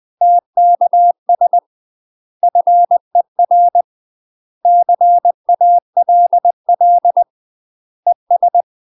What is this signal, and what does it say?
Morse code, heard as a keyed tone, about 20 wpm, 700 Hz. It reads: TKS FER CALL ES
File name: UK MASTER QSOs TRAINING / 20WPM / UK MASTER QSOs 0003